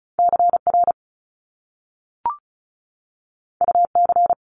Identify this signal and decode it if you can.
Morse code, heard as a keyed tone, about 36 wpm, 700 Hz. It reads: CR UC
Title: Top Two Letter Combinations in Words 35wpm